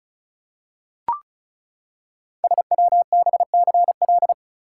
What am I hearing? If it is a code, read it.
Morse code, heard as a keyed tone, about 36 wpm, 690 Hz. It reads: SWBCL
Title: Code Groups - 5 Letters - 35wpm